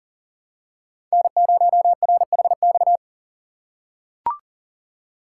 Morse code, heard as a keyed tone, about 40 wpm, 700 Hz. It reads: N0RHX
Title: USA Call Signs - 5 Characters - 4x Character Spacing - 40wpm